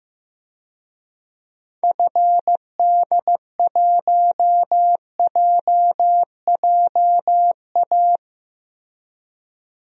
Morse code, heard as a keyed tone, about 15 wpm, 700 Hz. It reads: FD1JJA
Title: DXCC-100 Call Signs - 6 Characters - Repeated 3x - 15wpm